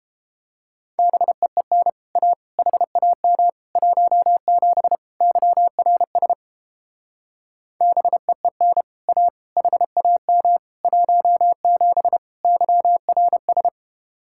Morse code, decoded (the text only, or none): BEEN A HAM 17 YRS BEEN A HAM 17 YRS